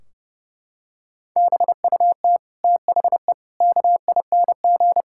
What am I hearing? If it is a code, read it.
Morse code, heard as a keyed tone, about 32 wpm, 700 Hz. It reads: BUT THE KING
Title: Sentences from Top 1600 Words 30wpm